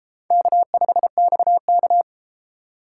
Morse code, heard as a keyed tone, about 31 wpm, 700 Hz. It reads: K5XK